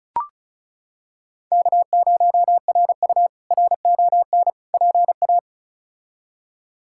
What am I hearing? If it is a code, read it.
Morse code, heard as a keyed tone, about 36 wpm, 700 Hz. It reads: K0RU RON PA
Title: K1USN SST QSO Practice - 35wpm